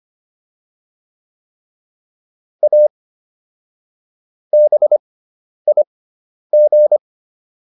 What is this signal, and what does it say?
Morse code, heard as a keyed tone, about 25 wpm, spaced as for 10 wpm, 600 Hz. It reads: A BIG